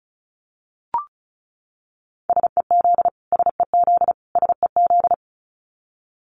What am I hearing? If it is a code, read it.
Morse code, heard as a keyed tone, about 34 wpm, 700 Hz. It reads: SEZ SEZ SEZ
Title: Abbreviations Repeated 3x - 35wpm